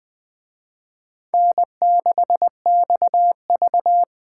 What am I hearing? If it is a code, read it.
Morse code, heard as a keyed tone, about 21 wpm, 700 Hz. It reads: N6XV